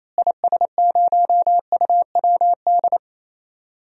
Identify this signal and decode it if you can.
Morse code, heard as a keyed tone, about 28 wpm, 700 Hz. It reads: IS0UWD
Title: DXCC-100 Call Signs - 6 Characters - Repeated 2x - 28wpm